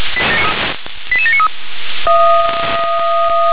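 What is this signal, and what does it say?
Five-tone selective call (ZVEI), heard as quick runs of five tones, tones of about 70 ms. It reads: 88793
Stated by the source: ZVEI five-tone selective call received on SDR.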